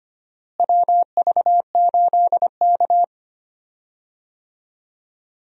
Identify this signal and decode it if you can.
Morse code, heard as a keyed tone, about 25 wpm, 700 Hz. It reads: WV8K
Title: USA Call Signs - 4 Characters - Repeated 3x - 25wpm